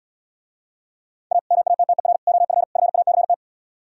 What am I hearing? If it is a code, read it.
Morse code, heard as a keyed tone, about 77 wpm, 700 Hz. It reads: A MATTER OF FACT